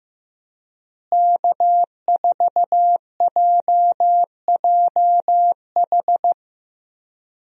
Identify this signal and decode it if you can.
Morse code, heard as a keyed tone, about 16 wpm, 700 Hz. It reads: K4JJH
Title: USA Call Signs - 5 Characters - Repeated 2x - 15wpm